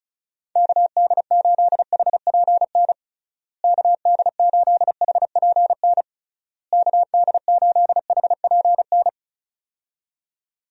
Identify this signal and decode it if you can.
Morse code, heard as a keyed tone, about 36 wpm, 700 Hz. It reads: KD8HPN KD8HPN KD8HPN
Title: USA Call Signs - 6 Characters - Repeated 3x - 35wpm